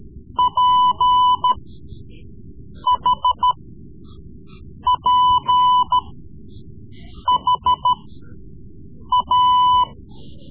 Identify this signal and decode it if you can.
Morse code, heard as a keyed tone, about 14 wpm, spaced as for 5 wpm, 1000 Hz. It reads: PHPHA
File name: Morse Lessons / Morse Lessons / Tape-3B